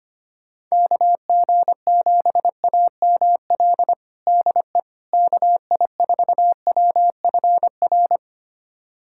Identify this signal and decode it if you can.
Morse code, heard as a keyed tone, about 25 wpm, 700 Hz. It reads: KG7AML DE KI4WFR
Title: USA Callsign DE Callsign 25wpm